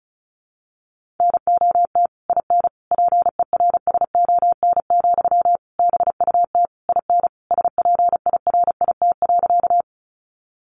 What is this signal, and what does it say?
Morse code, heard as a keyed tone, about 34 wpm, 700 Hz. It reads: NOT IN PERSON, BUT IN SPIRIT.
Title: Sentences from Top 1400 Words 35wpm